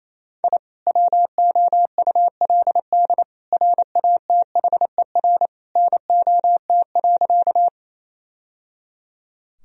Morse code, heard as a keyed tone, about 28 wpm, 700 Hz. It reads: I WOULD RATHER NOT.
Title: Sentences from Top 1000 Words - Review Part 4 - 28wpm